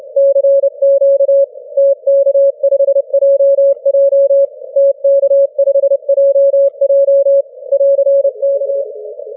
Morse code, heard as a keyed tone, about 29 wpm, 550 Hz. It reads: CQ TK5JJ TK5JJ <AR>K
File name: wp / tk5jj 6m cw 19 jun 2013 0633z